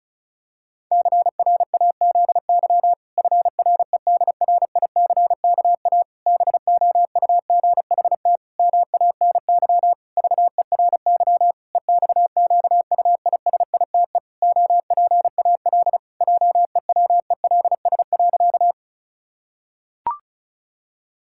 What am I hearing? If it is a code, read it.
Morse code, heard as a keyed tone, about 34 wpm, 700 Hz. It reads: CRAZY FREDRICKA BOUGHT MANY VERY EXQUISITE OPAL JEWELS.
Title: Pangram Challenge v2 - 35wpm